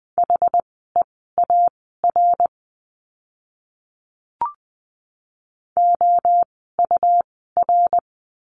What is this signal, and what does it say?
Morse code, heard as a keyed tone, about 20 wpm, spaced as for 14 wpm, 700 Hz. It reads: HEAR OUR
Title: 206 Lesson 33f - Letter j - Words 20wpm